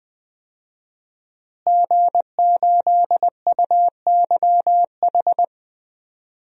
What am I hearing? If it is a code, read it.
Morse code, heard as a keyed tone, about 20 wpm, 700 Hz. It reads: G8UYH
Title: DXCC-100 Call Signs - 5 Characters - Repeated 3x - 20wpm